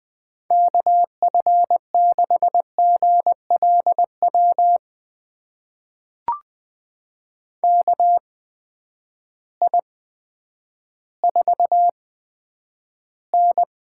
Morse code, heard as a keyed tone, about 20 wpm, spaced as for 10 wpm, 700 Hz. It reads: KF6GLW KI4N
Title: USA Call Signs - 6 Characters - 8x Character Spacing - 20wpm